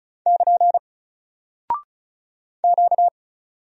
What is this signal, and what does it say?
Morse code, heard as a keyed tone, about 35 wpm, 700 Hz. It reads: <KN> Q
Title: Single Character (Letter:Number:Punctuation:Prosign) 35wpm